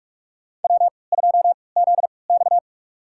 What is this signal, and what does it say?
Morse code, heard as a keyed tone, about 43 wpm, spaced as for 25 wpm, 700 Hz. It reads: W2ZX
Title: USA Call Signs - 3x Character Spacing - 45wpm